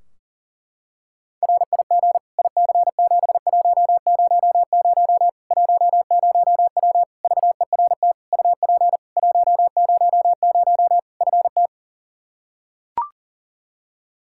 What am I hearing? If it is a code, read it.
Morse code, heard as a keyed tone, about 40 wpm, 700 Hz. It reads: RIG IC7100 10W VERT UP 100 FT
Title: Practice QSO - Exchange 2 - Info - 40wpm